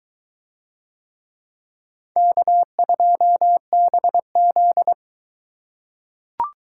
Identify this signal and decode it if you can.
Morse code, heard as a keyed tone, about 23 wpm, 700 Hz. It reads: K2BZ